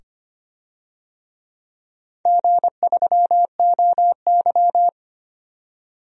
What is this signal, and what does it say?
Morse code, heard as a keyed tone, about 25 wpm, 700 Hz. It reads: G3OY
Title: DXCC-100 Call Signs - 4 Characters - 25wpm